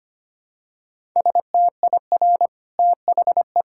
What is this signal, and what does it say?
Morse code, heard as a keyed tone, about 25 wpm, 700 Hz. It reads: STIR THE
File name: Easy Sentences with Hints - 25wpm